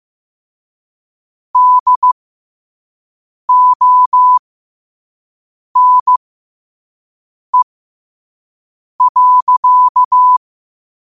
Morse code, heard as a keyed tone, about 15 wpm, spaced as for 5 wpm, 1000 Hz. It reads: DONE.
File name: qotdfiles / 5wpm